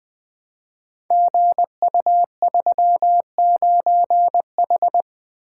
Morse code, heard as a keyed tone, about 20 wpm, 700 Hz. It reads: GU39H
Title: Code Groups - 5 Letter:Numbers - 20wpm